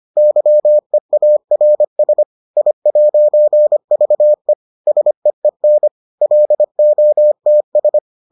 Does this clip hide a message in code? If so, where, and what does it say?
Yes: Morse code from 0.2 to 8.0 s, heard as a keyed tone, about 25 wpm, 600 Hz. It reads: YEARS I'VE SEEN LOTS